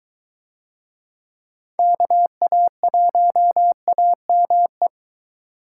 Morse code, heard as a keyed tone, about 22 wpm, 700 Hz. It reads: KA1AME